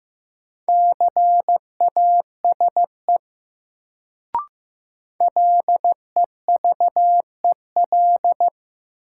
Morse code, heard as a keyed tone, about 15 wpm, 700 Hz. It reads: CASE LEVEL
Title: Sentences from Top 700 Words - New Word Drill - 15wpm